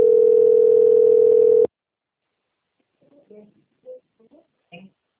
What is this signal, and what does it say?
Ringback tone (North America), 440 + 480 Hz, with speech.
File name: out- - - - - .31009